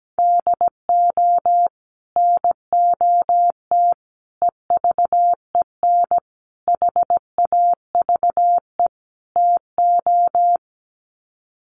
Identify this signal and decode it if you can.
Morse code, heard as a keyed tone, about 17 wpm, 700 Hz. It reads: DO NOT EVEN HAVE TO
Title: Sets of 5 Words Encore2 17wpm